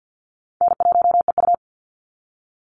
Morse code, heard as a keyed tone, about 48 wpm, 700 Hz. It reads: N1EU
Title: NA SOTA Activator Callsigns 50wpm